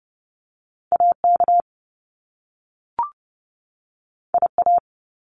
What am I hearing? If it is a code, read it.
Morse code, heard as a keyed tone, about 30 wpm, 700 Hz. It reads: AK IA
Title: US States 30wpm